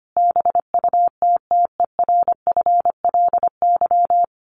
Morse code, heard as a keyed tone, about 25 wpm, 700 Hz. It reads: BUTTERFLY
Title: A Sound Of Thunder 25wpm